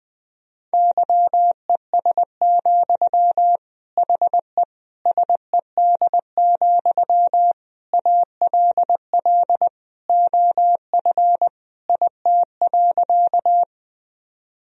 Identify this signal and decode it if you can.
Morse code, heard as a keyed tone, about 20 wpm, 700 Hz. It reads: YES, HE SED, ALL OF IT.
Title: Sentences from Top 500 Words and Abbreviations 20wpm